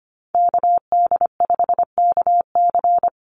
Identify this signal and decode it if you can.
Morse code, heard as a keyed tone, about 25 wpm, 690 Hz. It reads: KD5KC
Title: NA SOTA Activator Callsigns 25wpm